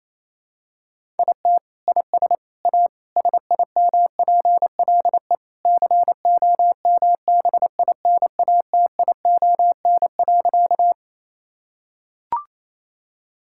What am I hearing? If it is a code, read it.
Morse code, heard as a keyed tone, about 28 wpm, 700 Hz. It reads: IT IS A SIMPLE COMBINATION.
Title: Sentences from Top 1000 Words - Review Part 6 - 28wpm